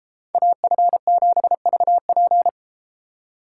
Morse code, heard as a keyed tone, about 31 wpm, 700 Hz. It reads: AF7VP